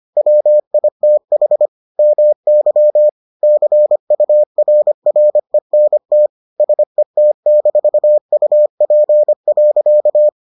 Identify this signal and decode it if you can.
Morse code, heard as a keyed tone, about 25 wpm, 600 Hz. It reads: WITH MY CURRENT SET-UP.